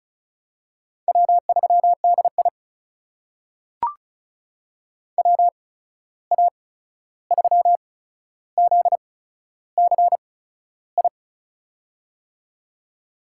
Morse code, heard as a keyed tone, about 36 wpm, spaced as for 13 wpm, 700 Hz. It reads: W3DI WA3ZCI
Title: USA Call Signs - 8x Character Spacing - 35wpm